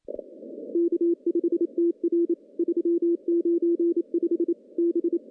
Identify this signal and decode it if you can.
Morse code, heard as a keyed tone, about 27 wpm, 340 Hz. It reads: K5TR 395 B
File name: contests / CW-SS-k5tr- -0600